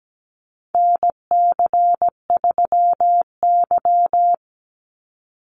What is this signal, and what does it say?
Morse code, heard as a keyed tone, about 17 wpm, 700 Hz. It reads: NC3Y
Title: Call Signs Repeated 3x - 17wpm